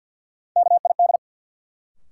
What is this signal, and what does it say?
Morse code, heard as a keyed tone, about 53 wpm, 700 Hz. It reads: KID
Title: Sentences from Top 1600 Words - New Word Drill Alt - 50wpm